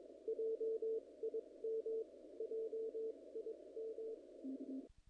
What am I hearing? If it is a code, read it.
Morse code, heard as a keyed tone, about 22 wpm, spaced as for 19 wpm, 450 Hz. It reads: JIM JIM
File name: contests / naqp-cw-k5tr-op-k5pi- -1800